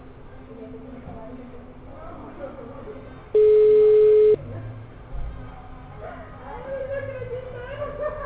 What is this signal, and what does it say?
Ringback tone (Europe, 425 Hz), with speech.
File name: out- -2004- - - .28211